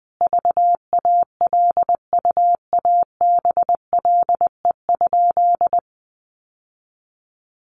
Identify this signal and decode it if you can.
Morse code, heard as a keyed tone, about 20 wpm, 700 Hz. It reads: VALUABLE?
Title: Sentences from Top 1100 Words 20wpm